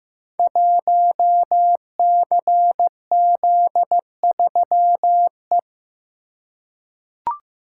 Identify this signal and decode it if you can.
Morse code, heard as a keyed tone, about 15 wpm, 700 Hz. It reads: 1CZ3E
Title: Code Groups - 5 Letter:Numbers - 15wpm